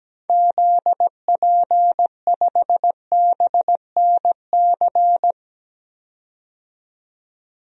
Morse code, heard as a keyed tone, about 17 wpm, 700 Hz. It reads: ZP5BNC
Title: DXCC-100 Call Signs - 6 Characters - 17wpm